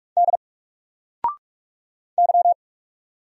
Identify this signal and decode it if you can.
Morse code, heard as a keyed tone, about 44 wpm, 700 Hz. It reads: D Y
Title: Single Character (Letter:Number:Punctuation:Prosign) 45wpm